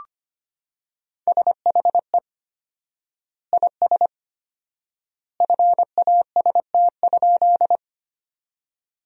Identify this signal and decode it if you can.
Morse code, heard as a keyed tone, about 25 wpm, 700 Hz. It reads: SHE IS FAST?
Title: 4x Word-Spacing - Sentences from Top 400 Words - 25wpm